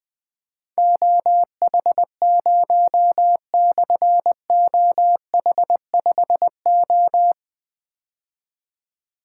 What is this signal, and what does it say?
Morse code, heard as a keyed tone, about 21 wpm, 700 Hz. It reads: OH0/OH5O
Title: Difficult Call Signs 20wpm